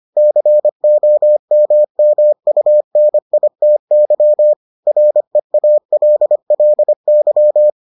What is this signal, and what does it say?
Morse code, heard as a keyed tone, about 25 wpm, 600 Hz. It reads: COMMUNITY REALLY